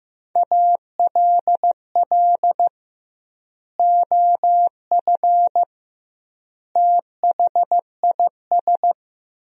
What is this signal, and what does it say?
Morse code, heard as a keyed tone, about 15 wpm, 700 Hz. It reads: ALL OF THIS
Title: 2x Word-Spacing - Sentences from Top 500 Words - 15wpm